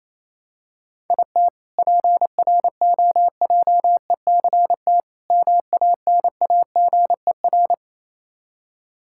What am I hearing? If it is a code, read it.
Morse code, heard as a keyed tone, about 28 wpm, 700 Hz. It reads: IT PROJECT MANAGER
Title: K1USN SST QSO Practice 28wpm